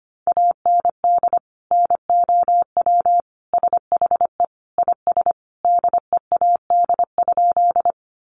Morse code, heard as a keyed tone, about 25 wpm, 700 Hz. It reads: AND NOW SHE IS DEAD?
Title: Sentences from Top 1500 Words 25wpm